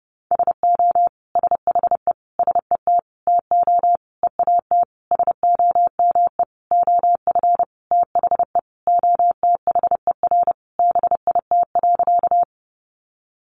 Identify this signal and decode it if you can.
Morse code, heard as a keyed tone, about 30 wpm, 700 Hz. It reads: SO SHE SET TO EAT SOME OF THE OTHER BIT.
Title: Sentences from Top 900 Words 30wpm